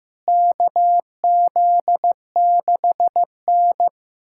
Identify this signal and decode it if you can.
Morse code, heard as a keyed tone, about 15 wpm, 700 Hz. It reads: KZ6N
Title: USA Callsign DE Callsign - 4 Characters - 15wpm